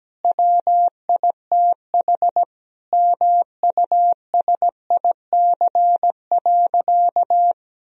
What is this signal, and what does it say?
Morse code, heard as a keyed tone, about 17 wpm, 700 Hz. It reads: WITH MUSIC.
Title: Sentences from Top 1000 Words - Review Part 1 - 17wpm